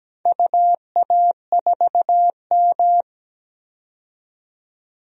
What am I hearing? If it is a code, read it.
Morse code, heard as a keyed tone, about 17 wpm, 700 Hz. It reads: UA4M
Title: DXCC-100 Call Signs - 4 Characters - 17wpm